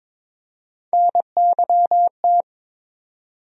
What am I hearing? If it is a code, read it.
Morse code, heard as a keyed tone, about 22 wpm, 700 Hz. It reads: NYT
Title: USA Call Sign Postfixes - 3 Letters 22wpm